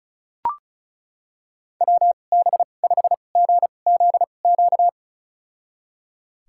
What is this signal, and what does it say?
Morse code, heard as a keyed tone, about 36 wpm, spaced as for 25 wpm, 700 Hz. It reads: WB5GZQ
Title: USA Call Signs - 6 Characters - 2x Character Spacing - 35wpm